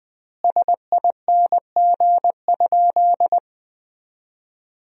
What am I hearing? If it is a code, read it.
Morse code, heard as a keyed tone, about 20 wpm, 700 Hz. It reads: SING?
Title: 3x Word-Spacing - Sentences from Top 400 Words - 20wpm